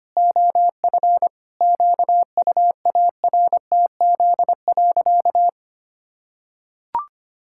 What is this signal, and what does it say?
Morse code, heard as a keyed tone, about 25 wpm, 700 Hz. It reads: OF QUARTZ.
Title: Pangram Challenge v2 - 25wpm